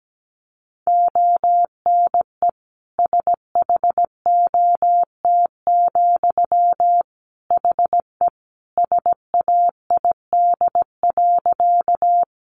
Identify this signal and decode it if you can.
Morse code, heard as a keyed tone, about 17 wpm, 700 Hz. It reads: ONE SHOT, HE SAID.
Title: Sentences from Top 1100 Words 17wpm